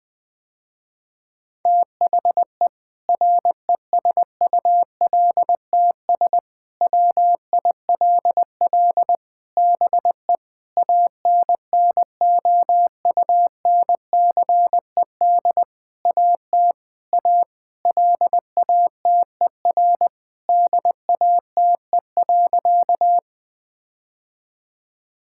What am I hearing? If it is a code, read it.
Morse code, heard as a keyed tone, about 21 wpm, 700 Hz. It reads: THE RESULTS WILL BE ANNOUNCED AT A LATER DATE.
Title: Sentences with 7-Word Phrases - Spoken Beginning - 20wpm